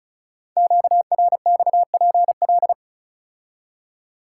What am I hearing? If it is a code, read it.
Morse code, heard as a keyed tone, about 36 wpm, 690 Hz. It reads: QRXPL
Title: Code Groups - 5 Letters - 35wpm